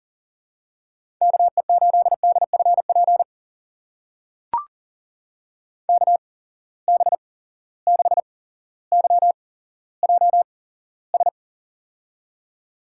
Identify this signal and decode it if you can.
Morse code, heard as a keyed tone, about 40 wpm, spaced as for 17 wpm, 700 Hz. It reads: KE8DFP KB6YJS
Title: USA Call Signs - 6 Characters - 8x Character Spacing - 40wpm